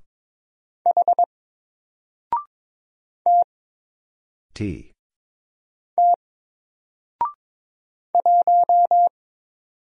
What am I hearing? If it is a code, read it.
Morse code, heard as a keyed tone, about 21 wpm, 700 Hz. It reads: H T T 1
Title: Single Letter-Number 22wpm